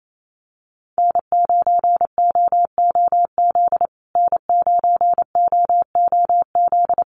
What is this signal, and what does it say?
Morse code, heard as a keyed tone, about 28 wpm, 700 Hz. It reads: N9OOZ N9OOZ
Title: USA Call Signs Repeated 3x 28wpm